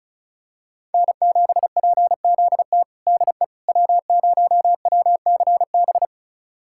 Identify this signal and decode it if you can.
Morse code, heard as a keyed tone, about 36 wpm, 700 Hz. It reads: N7PZT DE W0WCB
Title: USA Callsign DE Callsign 35wpm